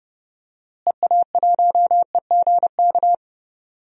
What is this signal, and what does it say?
Morse code, heard as a keyed tone, about 29 wpm, 700 Hz. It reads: EA1EGK
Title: DXCC-100 Call Signs - 6 Characters - Repeated 2x - 30wpm